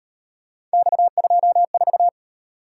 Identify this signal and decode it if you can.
Morse code, heard as a keyed tone, about 37 wpm, 700 Hz. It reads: X24